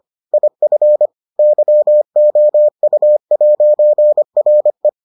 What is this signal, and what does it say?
Morse code, heard as a keyed tone, about 25 wpm, 600 Hz. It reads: IF YOU'RE